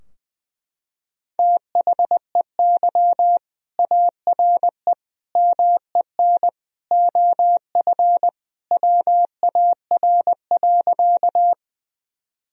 Morse code, heard as a keyed tone, about 20 wpm, 700 Hz. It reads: THEY ARE MEN OF WAR.
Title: Sentences from Top 1000 Words - Review Part 2 - 20wpm